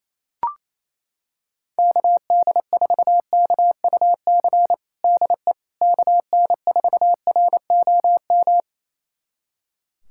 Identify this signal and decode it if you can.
Morse code, heard as a keyed tone, about 28 wpm, 700 Hz. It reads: KD4KUC DE KN4ROM
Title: USA Callsign DE Callsign 28wpm